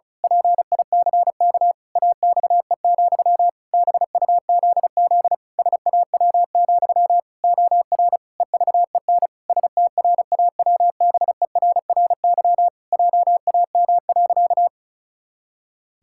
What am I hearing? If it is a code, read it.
Morse code, heard as a keyed tone, about 34 wpm, 700 Hz. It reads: PICK AXE, BUZZ SAW, OR EVEN STRAWBERRY JAM.
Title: Pangram Challenge v2 - 35wpm